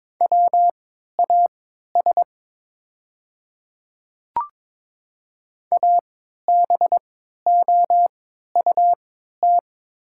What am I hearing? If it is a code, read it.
Morse code, heard as a keyed tone, about 22 wpm, spaced as for 13 wpm, 700 Hz. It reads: WAS ABOUT
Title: 3x Character Spacing - Top 200 Words - 22wpm